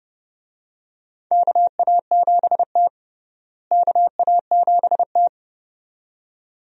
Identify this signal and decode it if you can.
Morse code, heard as a keyed tone, about 32 wpm, 700 Hz. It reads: KA7T KA7T
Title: USA Call Signs - 4 Characters - Repeated 2x - 30wpm